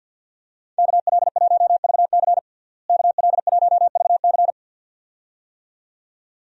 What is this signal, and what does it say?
Morse code, heard as a keyed tone, about 51 wpm, 700 Hz. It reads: KL1VC KL1VC
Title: USA Call Signs - 5 Characters - Repeated 2x - 50wpm